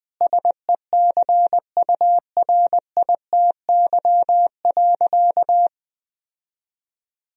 Morse code, heard as a keyed tone, about 21 wpm, 700 Hz. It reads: SECURITY.
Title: Sentences from Top 1000 Words - Review Part 4 - 20wpm